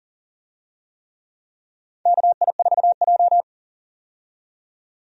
Morse code, heard as a keyed tone, about 37 wpm, 700 Hz. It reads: KI4J
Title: USA Call Signs - 4 Characters - 40wpm